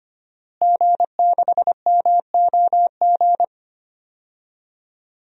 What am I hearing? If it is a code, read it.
Morse code, heard as a keyed tone, about 25 wpm, 700 Hz. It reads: G6MOG
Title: DXCC-100 Call Signs - 5 Characters - Repeated 3x - 25wpm